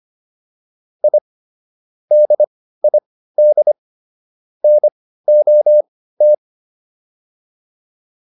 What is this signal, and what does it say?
Morse code, heard as a keyed tone, about 25 wpm, spaced as for 15 wpm, 600 Hz. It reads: I DID NOT